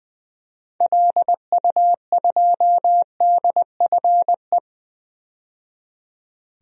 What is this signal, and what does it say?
Morse code, heard as a keyed tone, about 20 wpm, 700 Hz. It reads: LU2DFE
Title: DXCC-100 Call Signs - 6 Characters - 20wpm